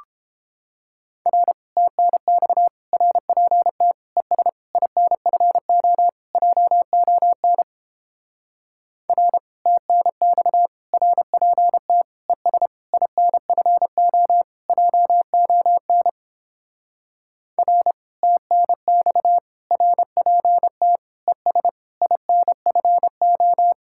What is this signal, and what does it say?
Morse code, heard as a keyed tone, about 29 wpm, 700 Hz. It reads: R TNX RPT ES INFO JON R TNX RPT ES INFO JON R TNX RPT ES INFO